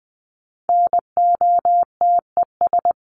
Morse code, heard as a keyed tone, about 20 wpm, 700 Hz. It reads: NOTES
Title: ES:S Suffix - Top 100 Words with Base Words 20wpm